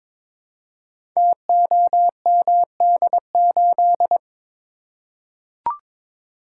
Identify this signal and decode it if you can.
Morse code, heard as a keyed tone, about 22 wpm, 700 Hz. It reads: TOMD8
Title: Code Groups - 5 Letter:Numbers - 22wpm